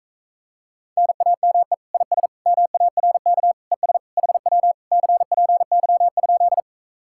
Morse code, heard as a keyed tone, about 44 wpm, 700 Hz. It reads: NAME IS MARK ES HW CPY?